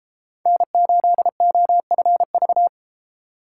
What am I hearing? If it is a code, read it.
Morse code, heard as a keyed tone, about 32 wpm, 700 Hz. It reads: N8OFV